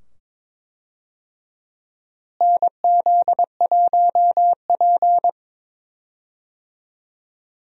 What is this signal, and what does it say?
Morse code, heard as a keyed tone, about 21 wpm, 700 Hz. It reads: NZ1P
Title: USA Call Signs - 4 Characters - Repeated 2x - 22wpm